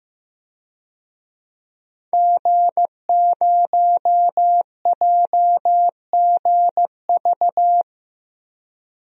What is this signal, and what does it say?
Morse code, heard as a keyed tone, about 16 wpm, 700 Hz. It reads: G0JGV
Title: DXCC-100 Call Signs - 5 Characters - Repeated 3x - 15wpm